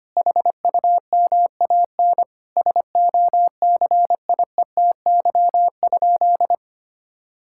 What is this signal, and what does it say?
Morse code, heard as a keyed tone, about 25 wpm, 700 Hz. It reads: HUMAN SOCIETY?
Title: Sentences from Top 1000 Words - Review Part 6 - 25wpm